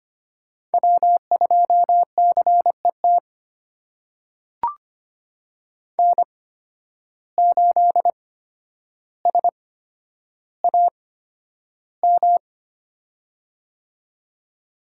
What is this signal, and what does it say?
Morse code, heard as a keyed tone, about 25 wpm, spaced as for 11 wpm, 700 Hz. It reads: W2CET N8SAM
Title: USA Call Signs - 5 Characters - 8x Character Spacing - 25wpm